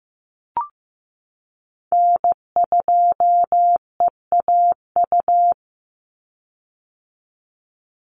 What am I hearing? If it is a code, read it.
Morse code, heard as a keyed tone, about 15 wpm, 700 Hz. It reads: N2EAU
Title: USA Call Signs 15wpm